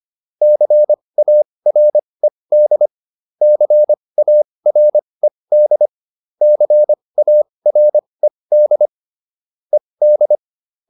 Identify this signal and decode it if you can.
Morse code, heard as a keyed tone, about 25 wpm, spaced as for 20 wpm, 600 Hz. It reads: CARED CARED CARED ED